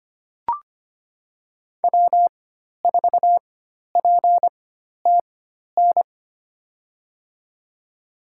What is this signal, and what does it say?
Morse code, heard as a keyed tone, about 25 wpm, spaced as for 12 wpm, 700 Hz. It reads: W4PTN
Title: USA Call Signs - 5 Characters - 4x Character Spacing - 25wpm